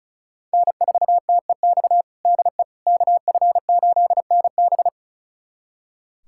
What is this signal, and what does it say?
Morse code, heard as a keyed tone, about 36 wpm, 700 Hz. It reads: N4TEX DE KF8NB
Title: USA Callsign DE Callsign - 5 Characters - 35wpm